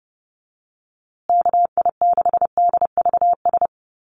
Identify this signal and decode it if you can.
Morse code, heard as a keyed tone, about 28 wpm, 700 Hz. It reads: KI6DVS
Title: USA Call Signs Repeated 2x 30wpm